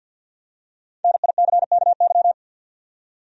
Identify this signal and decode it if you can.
Morse code, heard as a keyed tone, about 50 wpm, 700 Hz. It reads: NICKY
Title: USA Common Names - Max 5 Letters - 50wpm